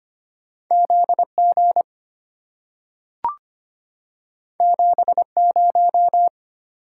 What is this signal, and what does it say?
Morse code, heard as a keyed tone, about 25 wpm, 700 Hz. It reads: ZG 70
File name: Two Letter-Number Combinations - 25wpm